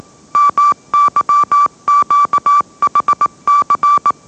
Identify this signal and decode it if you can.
Morse code, heard as a keyed tone, about 18 wpm, 1160 Hz. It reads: MYQHC
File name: mrx / mrx082 20wpm